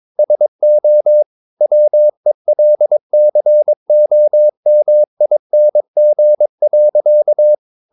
Morse code, heard as a keyed tone, about 22 wpm, 600 Hz. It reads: SO WELCOMING.
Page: Romanen Never af Ken Follet på engelsk.